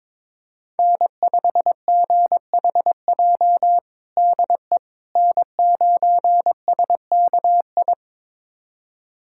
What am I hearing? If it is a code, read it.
Morse code, heard as a keyed tone, about 22 wpm, 700 Hz. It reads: N5GHJ DE N9SKI
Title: USA Callsign DE Callsign - 5 Characters - 22wpm